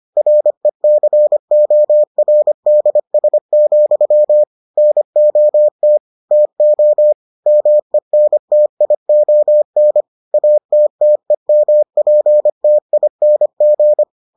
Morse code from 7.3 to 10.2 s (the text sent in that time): MENTION